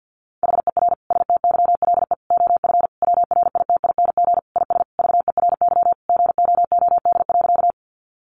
Morse code, heard as a keyed tone, about 50 wpm, 700 Hz. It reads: HER STYLE OF WRITING IS VERY GOOD.
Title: Sentences from Top 800 Words 50wpm